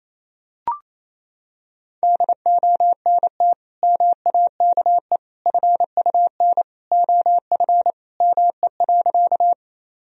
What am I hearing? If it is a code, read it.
Morse code, heard as a keyed tone, about 28 wpm, 700 Hz. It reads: DONT MAKE FUN OF ME.
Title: Sentences from Top 1000 Words - Review Part 4 - 28wpm